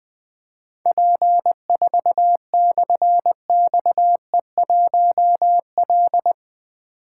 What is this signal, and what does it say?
Morse code, heard as a keyed tone, about 20 wpm, 700 Hz. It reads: P4/XE1L
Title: Difficult Call Signs 20wpm